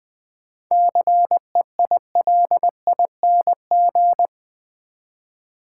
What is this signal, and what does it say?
Morse code, heard as a keyed tone, about 20 wpm, 700 Hz. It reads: CEILING
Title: 7 Letter Words - First 3 Letters Given - 20wpm